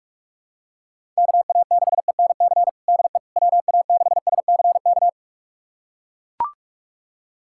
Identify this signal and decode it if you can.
Morse code, heard as a keyed tone, about 41 wpm, 700 Hz. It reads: KA6ENC DE WA6SCK
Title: USA Callsign DE Callsign - 6 Characters - 45wpm